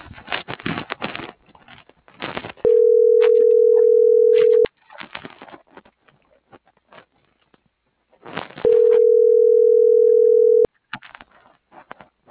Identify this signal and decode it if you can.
Ringback tone (North America), 440 + 480 Hz.